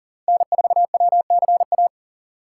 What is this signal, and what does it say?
Morse code, heard as a keyed tone, about 40 wpm, 700 Hz. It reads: N4WCA
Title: USA Call Signs - 5 Characters - 2x Character Spacing - 40wpm